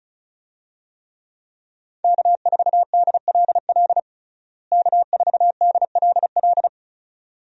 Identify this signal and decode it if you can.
Morse code, heard as a keyed tone, about 36 wpm, 700 Hz. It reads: K4DLL K4DLL
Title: USA Call Signs - 5 Characters - Repeated 2x - 35wpm